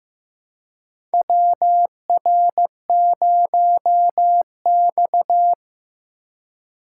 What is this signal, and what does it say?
Morse code, heard as a keyed tone, about 15 wpm, 700 Hz. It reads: WR0X
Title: USA Call Signs - 4 Characters - 4x Character Spacing - 15wpm